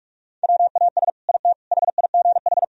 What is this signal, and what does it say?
Morse code, heard as a keyed tone, about 46 wpm, 700 Hz. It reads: WAS IT HIGH
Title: Sentences from Top 1000 Words - Review Part 4 - 45wpm